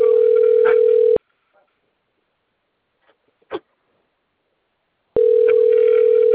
Ringback tone (North America), 440 + 480 Hz.